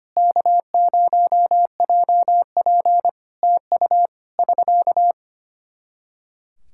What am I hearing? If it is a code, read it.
Morse code, heard as a keyed tone, about 25 wpm, 700 Hz. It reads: K0JP TU <SK>
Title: Practice QSO - Exchange 4 - Ending - 25wpm